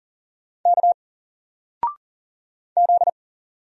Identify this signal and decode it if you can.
Morse code, heard as a keyed tone, about 40 wpm, 700 Hz. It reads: K Z
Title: Single Letter-Number 40wpm